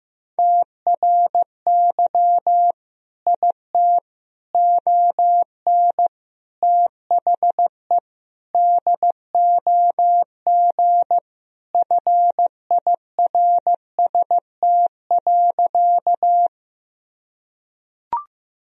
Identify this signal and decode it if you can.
Morse code, heard as a keyed tone, about 15 wpm, 700 Hz. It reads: TRY IT ON THE DOG FIRST.
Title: Sentences from Top 1000 Words - Review Part 2 - 15wpm